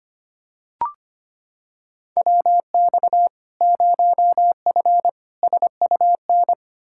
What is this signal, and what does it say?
Morse code, heard as a keyed tone, about 25 wpm, 700 Hz. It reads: WX 0F SUN
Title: Practice QSO - Exchange 2 - Info - 25wpm